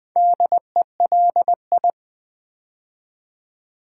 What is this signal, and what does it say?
Morse code, heard as a keyed tone, about 21 wpm, 700 Hz. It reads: DELI
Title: Build A Word - 4 Letters At A Time - 8-Letter Words - 20wpm